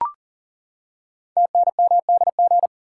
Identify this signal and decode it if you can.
Morse code, heard as a keyed tone, about 39 wpm, 700 Hz. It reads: TNMDG
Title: Code Groups - 5 Letters - 40wpm